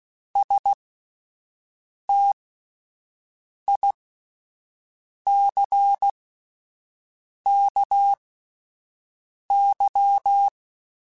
Morse code, heard as a keyed tone, about 16 wpm, spaced as for 5 wpm, 800 Hz. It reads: STICKY
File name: Misc / Misc-mp3s / Word Pairs @5